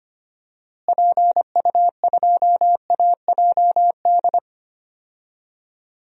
Morse code, heard as a keyed tone, about 25 wpm, 700 Hz. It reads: PU2AJD
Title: DXCC-100 Call Signs - 6 Characters - Repeated 3x - 25wpm